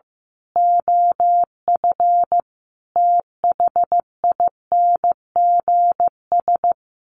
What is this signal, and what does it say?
Morse code, heard as a keyed tone, about 15 wpm, 700 Hz. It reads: OF THINGS
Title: Sets of 5 Words Encore2 15wpm